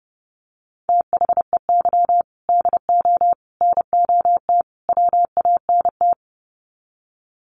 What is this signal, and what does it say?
Morse code, heard as a keyed tone, about 28 wpm, 700 Hz. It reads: THEY DO NOT WANT
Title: Sets of 4 Words 30wpm